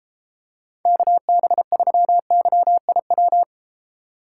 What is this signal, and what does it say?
Morse code, heard as a keyed tone, about 33 wpm, 700 Hz. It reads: KB3YIW